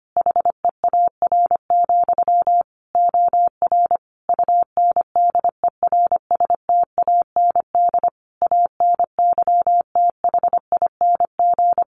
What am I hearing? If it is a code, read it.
Morse code, heard as a keyed tone, about 25 wpm, 700 Hz. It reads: HEAR, OR UNDERSTAND ANYTHING
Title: Sentences from Top 1300 Words 25wpm